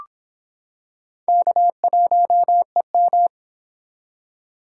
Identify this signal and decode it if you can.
Morse code, heard as a keyed tone, about 25 wpm, 700 Hz. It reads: K1EM